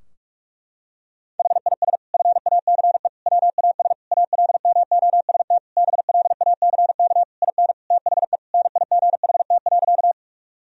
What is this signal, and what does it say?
Morse code, heard as a keyed tone, about 41 wpm, 700 Hz. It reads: HIS FACE WAS ALMOST BLACK IN THE NIGHT.
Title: Sentences from Top 1000 Words - Review Part 4 - 45wpm